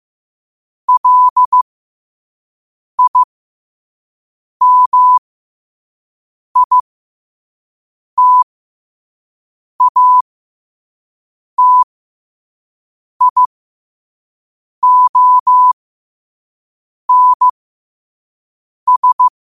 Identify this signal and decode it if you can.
Morse code, heard as a keyed tone, about 15 wpm, spaced as for 5 wpm, 1000 Hz. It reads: LIMITATIONS
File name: qotdfiles / 5wpm